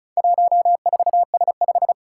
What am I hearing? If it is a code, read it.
Morse code, heard as a keyed tone, about 36 wpm, 700 Hz. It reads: 14S5
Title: Four Challenging Letter-Number Combinations 35wpm